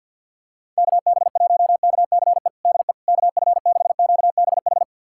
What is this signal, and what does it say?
Morse code, heard as a keyed tone, about 51 wpm, 700 Hz. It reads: KB1KCE DE KF6QBH
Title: USA Callsign DE Callsign 50wpm